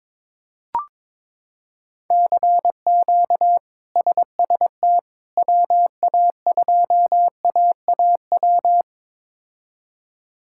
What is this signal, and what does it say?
Morse code, heard as a keyed tone, about 22 wpm, 700 Hz. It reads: CQ SST WA2AAW
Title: K1USN SST QSO Practice - 22wpm